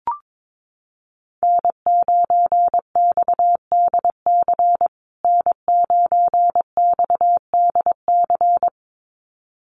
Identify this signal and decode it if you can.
Morse code, heard as a keyed tone, about 22 wpm, 700 Hz. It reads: N9XDC N9XDC
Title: USA Call Signs Repeated 2x 22wpm